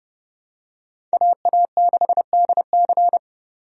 Morse code, heard as a keyed tone, about 29 wpm, 700 Hz. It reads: AA6DC
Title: USA Call Signs - 5 Characters - 30wpm